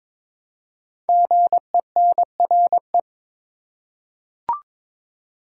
Morse code, heard as a keyed tone, about 22 wpm, 700 Hz. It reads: GENRE
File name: 5 Letter Words - First 3 Letters Given - 22wpm